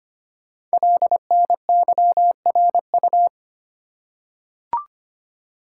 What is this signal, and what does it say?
Morse code, heard as a keyed tone, about 25 wpm, 700 Hz. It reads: LNYRU
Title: Code Groups - 5 Letter:Numbers - 25wpm